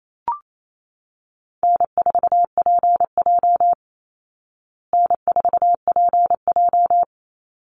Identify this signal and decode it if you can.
Morse code, heard as a keyed tone, about 28 wpm, 700 Hz. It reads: N4PJ N4PJ
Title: Easy Call Signs Repeated 2x - 28wpm